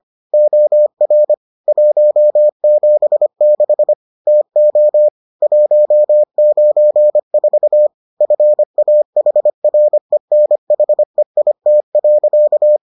Morse code, heard as a keyed tone, about 25 wpm, 600 Hz. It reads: OR 176 TO 194 FAHRENHEIT.